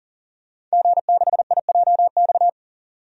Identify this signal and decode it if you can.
Morse code, heard as a keyed tone, about 37 wpm, 700 Hz. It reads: G6IJX
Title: DXCC-100 Call Signs - 5 Characters - 40wpm